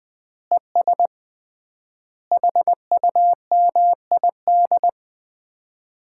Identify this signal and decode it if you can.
Morse code, heard as a keyed tone, about 20 wpm, 700 Hz. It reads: ES HUMID
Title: Common QSO Phrases - 3x Word-Spacing - 20wpm